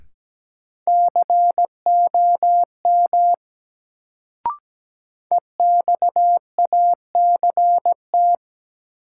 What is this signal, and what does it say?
Morse code, heard as a keyed tone, about 17 wpm, 700 Hz. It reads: COM EXACT
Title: Sentences from Top 1300 Words - New Word Drill - 17wpm